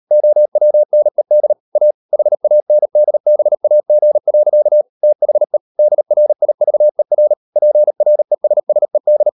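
Morse code, heard as a keyed tone, about 38 wpm, 600 Hz. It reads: OWNED A HANDBAG. THE DRIVER PRESSED
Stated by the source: Morse code practice at 38 WPM, sending Ken Follett's novel Never in English, from CW med Gnister.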